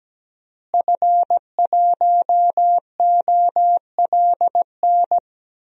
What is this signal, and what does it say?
Morse code, heard as a keyed tone, about 17 wpm, 700 Hz. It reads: F1OLN
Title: DXCC-100 Call Signs - 5 Characters - Repeated 3x - 17wpm